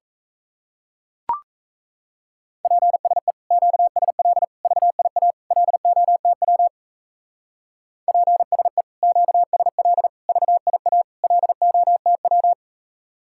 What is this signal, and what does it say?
Morse code, heard as a keyed tone, about 35 wpm, 700 Hz. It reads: PSE QSL VIA LOTW PSE QSL VIA LOTW